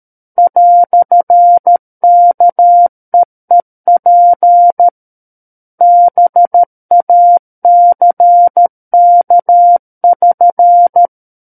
Morse code, heard as a keyed tone, about 13 wpm, 700 Hz. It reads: "KEEP BACK<SN>
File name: Book One / 04-Chapter 4 - 13 WPM